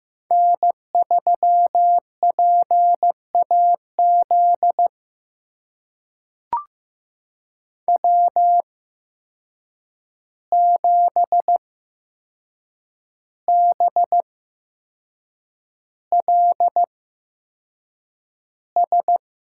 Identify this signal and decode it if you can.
Morse code, heard as a keyed tone, about 15 wpm, spaced as for 6 wpm, 700 Hz. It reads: N3PAZ W7BLS
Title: USA Call Signs - 5 Characters - 8x Character Spacing - 15wpm